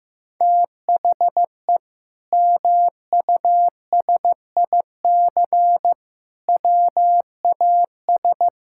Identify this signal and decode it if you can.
Morse code, heard as a keyed tone, about 15 wpm, 700 Hz. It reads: THE MUSIC WAS
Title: Sentences from Top 1000 Words - Review Part 6 - 15wpm